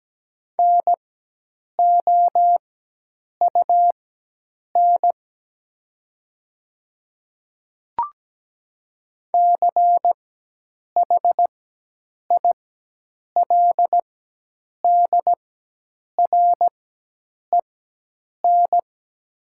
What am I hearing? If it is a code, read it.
Morse code, heard as a keyed tone, about 17 wpm, spaced as for 8 wpm, 700 Hz. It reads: NOUN CHILDREN
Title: 4x Character Spacing - Top 500 Words - 17wpm